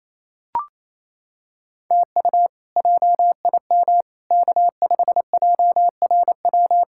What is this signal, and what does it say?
Morse code, heard as a keyed tone, about 28 wpm, 700 Hz. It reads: TU JIM K5JRW
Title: K1USN SST QSO Practice - 28wpm